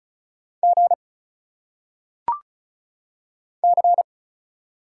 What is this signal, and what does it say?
Morse code, heard as a keyed tone, about 35 wpm, 700 Hz. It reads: G C
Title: Single Letters - Phonetic Pronunciation 35wpm